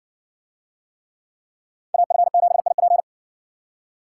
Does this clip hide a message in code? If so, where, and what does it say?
Yes: Morse code from 1.9 to 3.0 s, heard as a keyed tone, about 60 wpm, 700 Hz. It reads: AF7IP